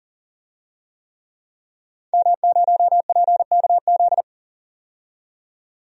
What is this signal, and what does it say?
Morse code, heard as a keyed tone, about 40 wpm, 700 Hz. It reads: M0PKZ
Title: DXCC-100 Call Signs - 5 Characters - 40wpm